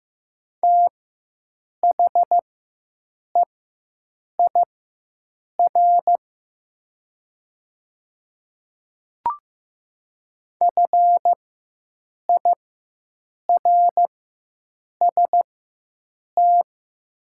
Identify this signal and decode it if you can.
Morse code, heard as a keyed tone, about 15 wpm, spaced as for 7 wpm, 700 Hz. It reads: THEIR FIRST
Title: 4x Character Spacing - Top 100 Words - 15wpm